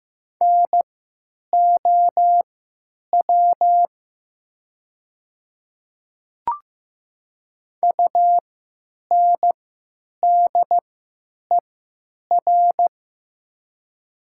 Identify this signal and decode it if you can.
Morse code, heard as a keyed tone, about 15 wpm, spaced as for 8 wpm, 700 Hz. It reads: NOW UNDER
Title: 3x Character Spacing - Top 200 Words - 15wpm